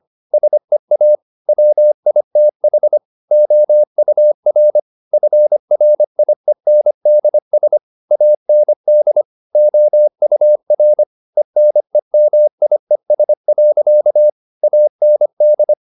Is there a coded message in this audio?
Yes: Morse code from 0.3 to 15.7 s, heard as a keyed tone, about 25 wpm, 600 Hz. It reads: SEA WITH OUR FRIENDS AND OUR ENEMIES. AND